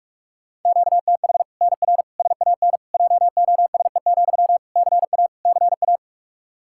Morse code, heard as a keyed tone, about 46 wpm, 700 Hz. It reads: QTH NR SAN JOSE, CA CA
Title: Practice QSO - Exchange 1 - Beginning - 45wpm